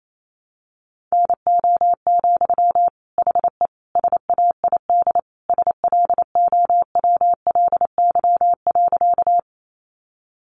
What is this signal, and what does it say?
Morse code, heard as a keyed tone, about 28 wpm, 700 Hz. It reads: NO, HE SAID SLOWLY.
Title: Sentences from Top 1100 Words 28wpm